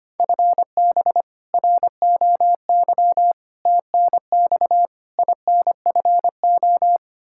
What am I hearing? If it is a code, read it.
Morse code, heard as a keyed tone, about 25 wpm, 700 Hz. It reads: FB ROY TNX INFO
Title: K1USN SST QSO Practice 25wpm